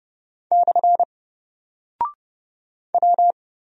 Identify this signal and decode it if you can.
Morse code, heard as a keyed tone, about 31 wpm, 700 Hz. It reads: / W
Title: Single Character (Letter:Number:Punctuation:Prosign) 30wpm